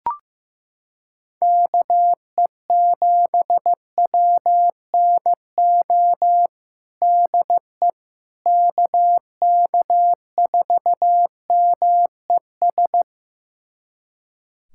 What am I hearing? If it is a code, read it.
Morse code, heard as a keyed tone, about 15 wpm, 700 Hz. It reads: KE7WNO DE KK4MES
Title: USA Callsign DE Callsign - 6 Characters - 15wpm